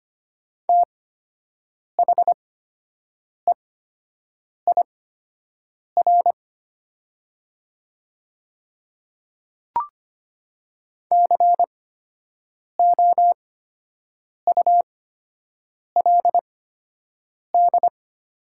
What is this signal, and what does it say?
Morse code, heard as a keyed tone, about 25 wpm, spaced as for 7 wpm, 700 Hz. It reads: THEIR COULD
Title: 8x Character Spacing - Top 500 Words - 25wpm